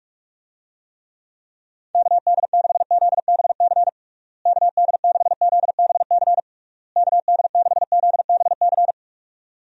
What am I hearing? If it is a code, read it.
Morse code, heard as a keyed tone, about 47 wpm, 700 Hz. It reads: KD6ZBC KD6ZBC KD6ZBC
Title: USA Call Signs - 6 Characters - Repeated 3x - 45wpm